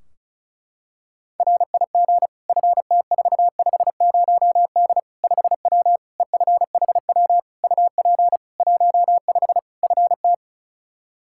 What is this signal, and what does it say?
Morse code, heard as a keyed tone, about 36 wpm, 700 Hz. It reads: RIG FT450D 5W EFHW UP 15 FT
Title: Practice QSO - Exchange 2 - Info - 35wpm